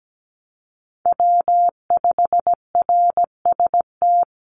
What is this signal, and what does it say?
Morse code, heard as a keyed tone, about 17 wpm, 700 Hz. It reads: W5RST
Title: NA SOTA Chaser Callsigns 17wpm